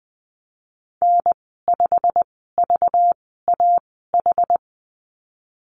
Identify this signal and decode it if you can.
Morse code, heard as a keyed tone, about 19 wpm, spaced as for 14 wpm, 700 Hz. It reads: N5VAH
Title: 193 Lesson 31g - Letter v - Call Signs 20wpm